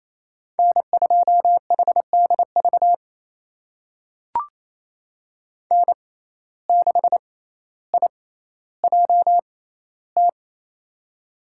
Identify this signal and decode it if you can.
Morse code, heard as a keyed tone, about 28 wpm, spaced as for 14 wpm, 700 Hz. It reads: N2HDV N6IJT
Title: USA Call Signs - 5 Characters - 6x Character Spacing - 28wpm